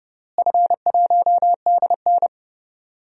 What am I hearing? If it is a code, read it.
Morse code, heard as a keyed tone, about 31 wpm, 700 Hz. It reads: F1DN
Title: DXCC-100 Call Signs - 4 Characters - Repeated 2x - 30wpm